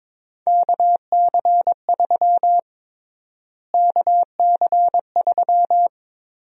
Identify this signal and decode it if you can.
Morse code, heard as a keyed tone, about 22 wpm, 700 Hz. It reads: KC3 KC3
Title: USA Call Sign Prefixes - Repeated 2x - 22wpm